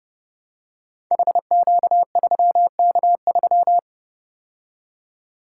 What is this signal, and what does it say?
Morse code, heard as a keyed tone, about 30 wpm, 700 Hz. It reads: HQ3K3
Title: Code Groups - 5 Letter:Numbers - 30wpm